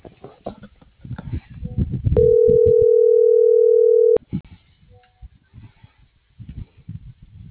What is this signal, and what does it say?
Ringback tone (North America), 440 + 480 Hz.